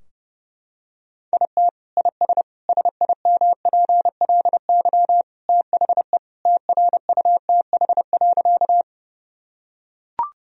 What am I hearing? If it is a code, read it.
Morse code, heard as a keyed tone, about 30 wpm, 700 Hz. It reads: IT IS SIMPLY THE TRUTH.
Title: Sentences from Top 1000 Words - Review Part 4 - 30wpm